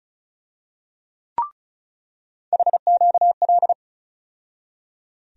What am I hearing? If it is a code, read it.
Morse code, heard as a keyed tone, about 36 wpm, 700 Hz. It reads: HQL
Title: USA Call Sign Postfixes - 3 Letters 35wpm